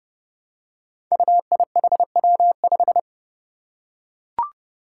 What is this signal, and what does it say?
Morse code, heard as a keyed tone, about 30 wpm, 700 Hz. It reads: UIHW5
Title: Five Challenging Letter-Number Combinations 30wpm